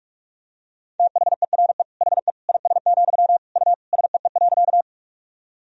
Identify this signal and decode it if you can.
Morse code, heard as a keyed tone, about 46 wpm, 700 Hz. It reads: THERE HE IS, U SEE.
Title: Sentences from Top 100 Words and Abbreviations 45wpm